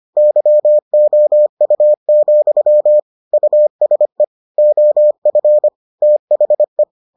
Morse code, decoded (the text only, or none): YOU, USE OF THE